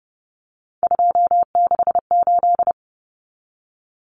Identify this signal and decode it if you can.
Morse code, heard as a keyed tone, about 30 wpm, 700 Hz. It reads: 268
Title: Common Cut Numbers - 3-Digit 30wpm